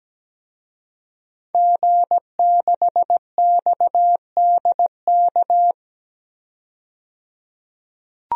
Morse code, heard as a keyed tone, about 17 wpm, 700 Hz. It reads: G6XDK
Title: DXCC-100 Call Signs - 5 Characters - 17wpm